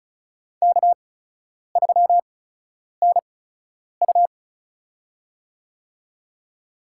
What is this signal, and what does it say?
Morse code, heard as a keyed tone, about 36 wpm, spaced as for 10 wpm, 700 Hz. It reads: K3NU
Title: USA Call Signs - 8x Character Spacing - 35wpm